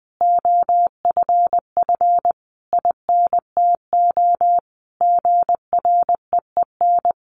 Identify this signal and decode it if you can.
Morse code, heard as a keyed tone, about 20 wpm, 700 Hz. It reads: OFF INTO GREEN
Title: A Sound Of Thunder 20wpm